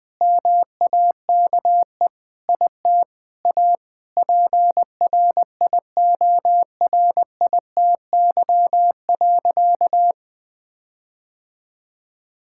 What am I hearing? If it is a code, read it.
Morse code, heard as a keyed tone, about 21 wpm, 700 Hz. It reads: MAKE IT A PRIORITY.
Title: Sentences from Top 1600 Words 20wpm